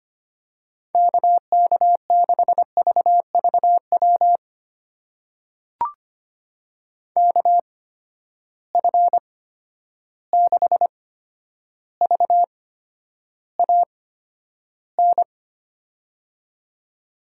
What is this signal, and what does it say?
Morse code, heard as a keyed tone, about 25 wpm, spaced as for 11 wpm, 700 Hz. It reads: KK6VVW KF6VAN
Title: USA Call Signs - 6 Characters - 8x Character Spacing - 25wpm